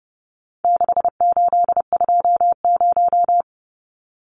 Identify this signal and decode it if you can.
Morse code, heard as a keyed tone, about 30 wpm, 700 Hz. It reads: 6820
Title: Numbers - 4-Digit 30wpm